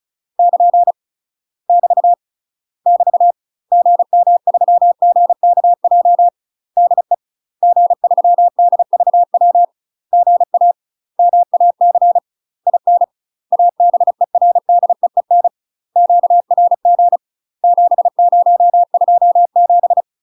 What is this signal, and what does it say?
Morse code, heard as a keyed tone, about 36 wpm, 700 Hz. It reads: <KN> <BT> <BT> GM3GKJ DE G3DVW GA MAC IN ABERDEEN QRG 7027
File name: UK MASTER QSOs TRAINING / 35WPM / UK MASTER QSOs 0018